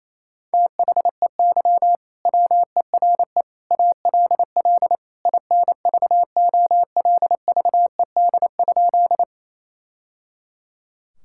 Morse code, heard as a keyed tone, about 28 wpm, 700 Hz. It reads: THEY WERE ALL INVOLVED?
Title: Sentences from Top 1000 Words - Review Part 5 - 28wpm